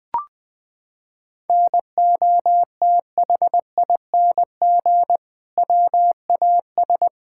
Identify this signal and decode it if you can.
Morse code, heard as a keyed tone, about 21 wpm, 700 Hz. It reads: NOTHING WAS
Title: Sentences from Top 1000 Words - Review Part 5 - 20wpm